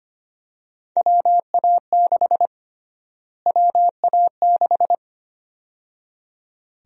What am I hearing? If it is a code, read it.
Morse code, heard as a keyed tone, about 25 wpm, 700 Hz. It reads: WA6 WA6
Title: USA Call Sign Prefixes - Repeated 2x - 25wpm